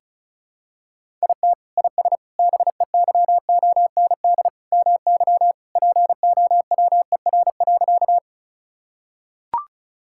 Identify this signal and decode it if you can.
Morse code, heard as a keyed tone, about 36 wpm, 700 Hz. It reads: IT IS BEYOND MY POWER.
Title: Sentences from Top 1000 Words - Review Part 6 - 35wpm